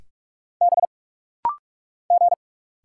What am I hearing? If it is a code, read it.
Morse code, heard as a keyed tone, about 44 wpm, 700 Hz. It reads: B G
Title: Single Letter-Number Alt - Rapid-Fire 45wpm